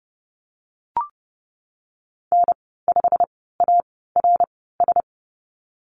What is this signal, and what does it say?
Morse code, heard as a keyed tone, about 30 wpm, spaced as for 17 wpm, 700 Hz. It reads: N5ARS
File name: course / 30:10wpm / 116 Lesson 19g - Letter w - Call Signs 30wpm